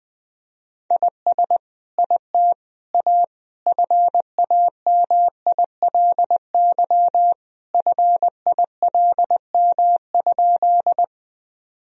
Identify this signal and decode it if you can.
Morse code, heard as a keyed tone, about 21 wpm, 700 Hz. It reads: IS IT A FAMILY FILM?
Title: Sentences from Top 1000 Words - Review Part 3 - 20wpm